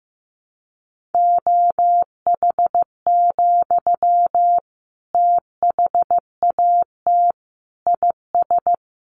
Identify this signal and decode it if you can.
Morse code, heard as a keyed tone, about 15 wpm, 700 Hz. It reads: OH, THAT IS
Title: Sentences from Top 700 Words 15wpm